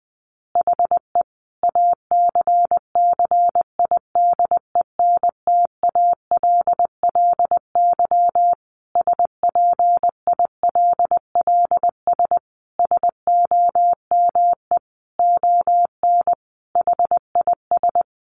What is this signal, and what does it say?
Morse code, heard as a keyed tone, about 20 wpm, 700 Hz. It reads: HE ACCIDENTALLY SPILLS SOME ON HIS
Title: Additional Easy Phrases 20wpm